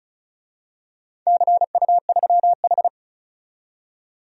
Morse code, heard as a keyed tone, about 34 wpm, 700 Hz. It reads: CU3H
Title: DXCC-100 Call Signs - 4 Characters - 35wpm